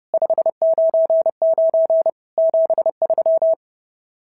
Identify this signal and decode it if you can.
Morse code, heard as a keyed tone, about 30 wpm, 650 Hz. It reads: 599 73
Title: SOTA QSO Practice 30wpm